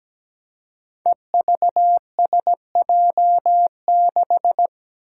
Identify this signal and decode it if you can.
Morse code, heard as a keyed tone, about 17 wpm, 700 Hz. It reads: EVSJ6
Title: Five Challenging Letter-Number Combinations 17wpm